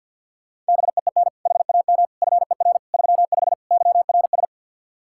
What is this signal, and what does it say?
Morse code, heard as a keyed tone, about 47 wpm, 700 Hz. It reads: BEEN HAM FER 35 YRS
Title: Practice QSO - Exchange 3 - More Info - 50wpm